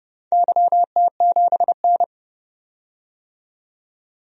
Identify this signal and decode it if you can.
Morse code, heard as a keyed tone, about 30 wpm, 700 Hz. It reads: YT7N
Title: DXCC-100 Call Signs - 4 Characters - 30wpm